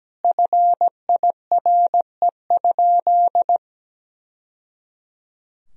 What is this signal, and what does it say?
Morse code, heard as a keyed tone, about 17 wpm, 700 Hz. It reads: FIRE?
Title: Sentences from Top 1000 Words - Review Part 3 - 17wpm